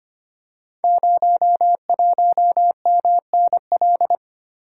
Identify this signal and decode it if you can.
Morse code, heard as a keyed tone, about 25 wpm, 700 Hz. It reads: 01MNL
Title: Code Groups - 5 Letter:Numbers - 25wpm